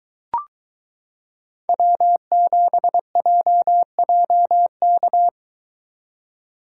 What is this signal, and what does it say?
Morse code, heard as a keyed tone, about 23 wpm, 700 Hz. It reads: W7JJK